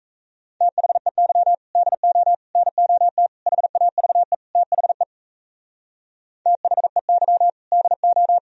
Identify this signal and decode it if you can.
Morse code, heard as a keyed tone, about 36 wpm, 700 Hz. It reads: THEY DO NOT HAVE THE THEY DO